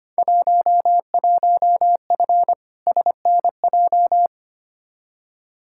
Morse code, heard as a keyed tone, about 25 wpm, 700 Hz. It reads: 11F SNJ
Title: Field Day Exchanges Encore - 25wpm